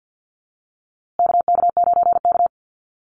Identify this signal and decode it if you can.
Morse code, heard as a keyed tone, about 50 wpm, 700 Hz. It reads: KK9K
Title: USA Call Signs Repeated 3x 50wpm